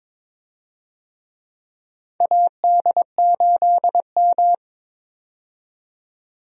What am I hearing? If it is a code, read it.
Morse code, heard as a keyed tone, about 22 wpm, 700 Hz. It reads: AD8M
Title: USA Call Signs - 4 Characters - 22wpm